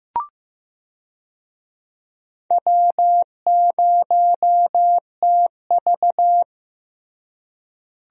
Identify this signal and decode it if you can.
Morse code, heard as a keyed tone, about 15 wpm, 700 Hz. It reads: W0TV
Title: USA Call Signs - 4 Characters - Repeated 3x - 15wpm